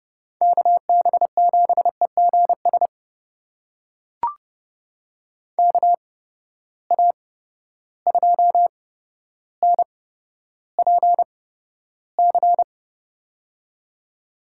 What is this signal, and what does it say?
Morse code, heard as a keyed tone, about 30 wpm, spaced as for 13 wpm, 700 Hz. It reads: KB7EGS KA2NPC
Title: USA Call Signs - 6 Characters - 8x Character Spacing - 30wpm